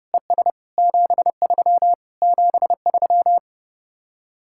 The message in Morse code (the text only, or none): ES 73 73